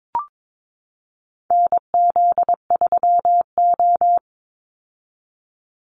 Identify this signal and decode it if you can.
Morse code, heard as a keyed tone, about 22 wpm, 700 Hz. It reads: NZ3O
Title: Call Signs - 22wpm